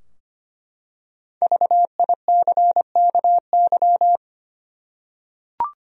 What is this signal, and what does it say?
Morse code, heard as a keyed tone, about 25 wpm, 700 Hz. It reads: VICKY
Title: USA Common Names - Max 5 Letters - 25wpm